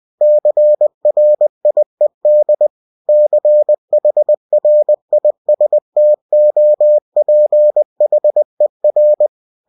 Morse code, heard as a keyed tone, about 20 wpm, 600 Hz. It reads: CRIED CHRISTOPHER
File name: cw-books / winnie-the-pooh / winnie-20wpm-page-0108